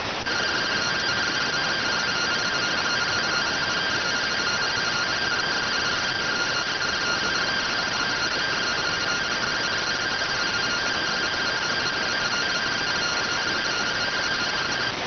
Two-tone teleprinter signal, about 170 Hz shift.